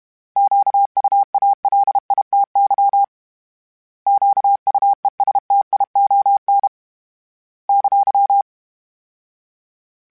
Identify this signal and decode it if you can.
Morse code, heard as a keyed tone, about 32 wpm, 800 Hz. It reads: QUALITY QUESTION !
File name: Misc / Misc-mp3s / Misc @30